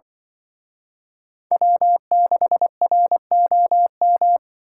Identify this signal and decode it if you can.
Morse code, heard as a keyed tone, about 23 wpm, 700 Hz. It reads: W6ROM
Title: Speed Racing - USA 1x3 Call Signs 20wpm